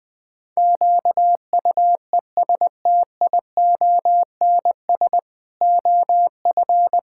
Morse code, heard as a keyed tone, about 20 wpm, 700 Hz. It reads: QUESTIONS OF
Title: Pangram Challenge v2 - 20wpm